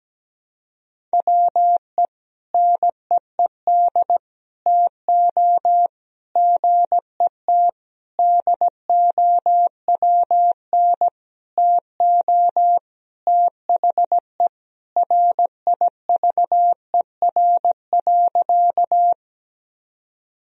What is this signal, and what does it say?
Morse code, heard as a keyed tone, about 17 wpm, 700 Hz. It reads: WE NEED TO GET DOWN TO THE RIVER.
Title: Sentences from Top 1000 Words - Review Part 1 - 17wpm